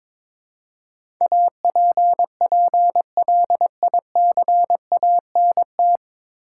Morse code, heard as a keyed tone, about 22 wpm, 700 Hz. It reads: APPLICANT
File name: 9 Letter Words - First 3 Letters Given - 22wpm